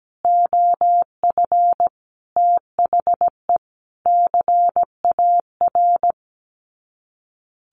Morse code, heard as a keyed tone, about 17 wpm, 700 Hz. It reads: OF THE CAR
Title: Sets of 5 Words Encore2 17wpm